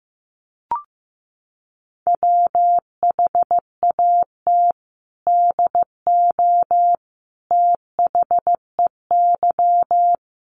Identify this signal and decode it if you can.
Morse code, heard as a keyed tone, about 14 wpm, 700 Hz. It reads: WHAT DO THEY
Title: Sentences from Top 1400 Words 15wpm